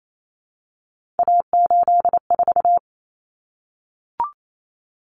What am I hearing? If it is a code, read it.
Morse code, heard as a keyed tone, about 28 wpm, 700 Hz. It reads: A84
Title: Common Cut Numbers - 3-Digit 28wpm